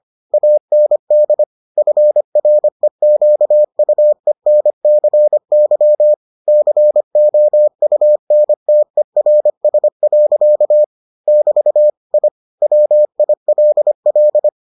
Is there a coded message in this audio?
Yes: Morse code from 0.3 to 14.5 s, heard as a keyed tone, about 25 wpm, 600 Hz. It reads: AND FREQUENCY COUNTERS. <BT> I WILL